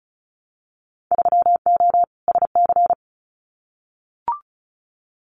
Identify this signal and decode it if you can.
Morse code, heard as a keyed tone, about 34 wpm, 700 Hz. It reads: 3O SC
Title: Winter Field Day Exchanges 35wpm